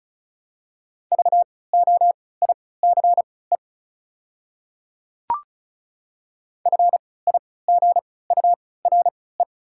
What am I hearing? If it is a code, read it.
Morse code, heard as a keyed tone, about 34 wpm, spaced as for 20 wpm, 700 Hz. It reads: VOICE FIGURE
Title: 3x Character Spacing - Top 500 Words - 35wpm